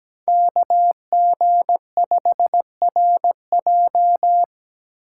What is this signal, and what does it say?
Morse code, heard as a keyed tone, about 21 wpm, 700 Hz. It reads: KG5RJ
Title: Speed Racing - USA 2x2 Call Signs 17wpm